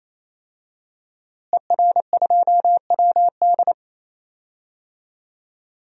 Morse code, heard as a keyed tone, about 28 wpm, 700 Hz. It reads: ER2WD
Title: DXCC-100 Call Signs - 5 Characters - Repeated 3x - 28wpm